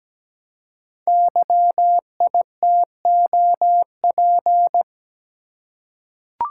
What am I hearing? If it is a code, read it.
Morse code, heard as a keyed tone, about 17 wpm, 700 Hz. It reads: YITOP
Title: Code Groups - 5 Letters - 17wpm